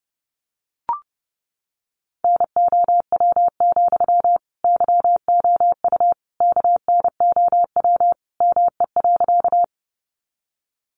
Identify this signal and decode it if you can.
Morse code, heard as a keyed tone, about 30 wpm, 700 Hz. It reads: NOW, YOU KNOW ME.
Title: Sentences from Top 200 Words 30wpm